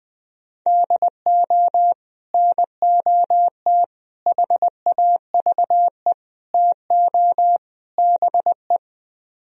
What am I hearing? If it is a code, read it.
Morse code, heard as a keyed tone, about 20 wpm, 700 Hz. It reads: DO NOT HAVE TO BE
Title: Speed-Racing - Sets of 5 Words 17wpm